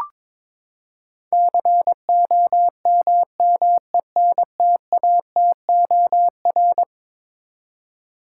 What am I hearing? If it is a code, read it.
Morse code, heard as a keyed tone, about 22 wpm, 700 Hz. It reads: COMMENTATOR
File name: practice / 22wpm / Eleven Letter Words - 22wpm